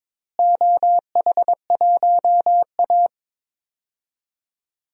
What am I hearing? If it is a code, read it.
Morse code, heard as a keyed tone, about 22 wpm, 700 Hz. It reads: OH1A
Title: DXCC-100 Call Signs - 4 Characters - Repeated 3x - 22wpm